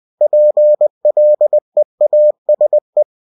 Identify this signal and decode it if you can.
Morse code, heard as a keyed tone, about 20 wpm, 600 Hz. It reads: PLEASE
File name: cw-books / winnie-the-pooh / winnie-20wpm-page-0122